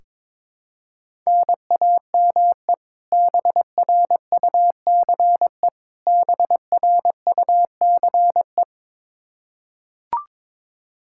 Morse code, heard as a keyed tone, about 22 wpm, 700 Hz. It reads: NAME BRUCE BRUCE
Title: Practice QSO - Exchange 1 - Beginning - 22wpm